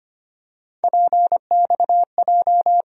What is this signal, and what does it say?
Morse code, heard as a keyed tone, about 25 wpm, 700 Hz. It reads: PXJ
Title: USA Call Sign Postfixes - 3 Letters 25wpm